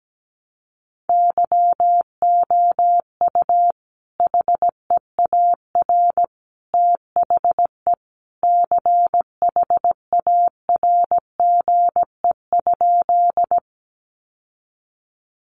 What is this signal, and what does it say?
Morse code, heard as a keyed tone, about 17 wpm, 700 Hz. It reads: YOU HEAR THE CHARGE?
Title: Sentences from Top 900 Words 17wpm